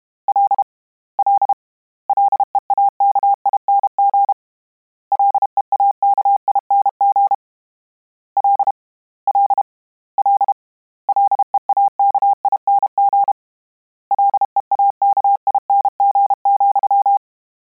Morse code, heard as a keyed tone, about 32 wpm, 800 Hz. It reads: L L LEAKING LEAKING L L L LEAKING LEAKING,
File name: Combinations / Combinations-mp3s / Combinations@30